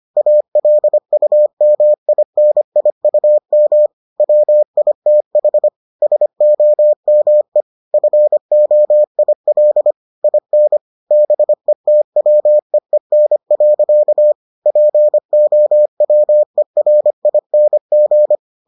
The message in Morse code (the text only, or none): ALUMINIUM WITH SOME FOIL IN BETWEEN. POWERING